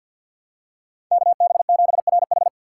Morse code, heard as a keyed tone, about 48 wpm, 700 Hz. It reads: KB7RH
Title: USA Call Signs - 5 Characters - 4x Character Spacing - 50wpm